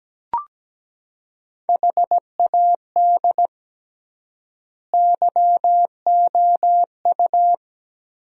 Morse code, heard as a keyed tone, about 17 wpm, 700 Hz. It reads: HAD YOU
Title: 3x Word-Spacing - Sentences from Top 400 Words - 17wpm